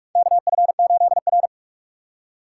Morse code, heard as a keyed tone, about 46 wpm, 690 Hz. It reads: KF8R
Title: USA Call Signs - 4 Characters - 2x Character Spacing - 45wpm